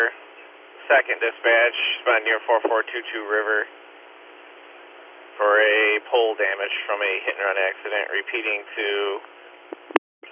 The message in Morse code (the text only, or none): none